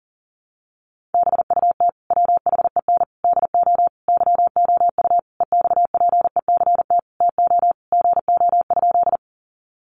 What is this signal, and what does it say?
Morse code, heard as a keyed tone, about 40 wpm, 700 Hz. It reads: BUT WHEN DO YOU EXPECT TO GO?
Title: Sentences from Top 1100 Words 40wpm